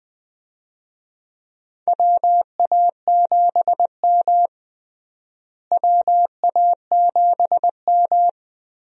Morse code, heard as a keyed tone, about 20 wpm, 700 Hz. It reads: WA7M WA7M
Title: USA Call Signs - 4 Characters - Repeated 2x - 20wpm